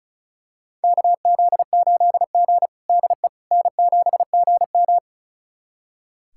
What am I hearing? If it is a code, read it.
Morse code, heard as a keyed tone, about 36 wpm, 700 Hz. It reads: KZ8G DE N7GM
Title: USA Callsign DE Callsign - 4 Characters - 35wpm